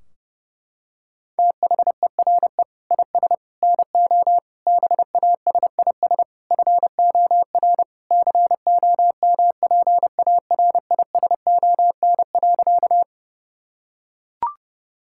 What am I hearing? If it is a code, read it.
Morse code, heard as a keyed tone, about 30 wpm, 700 Hz. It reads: THERE IS NO BASIS FOR COMPARISON.
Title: Sentences from Top 1600 Words 30wpm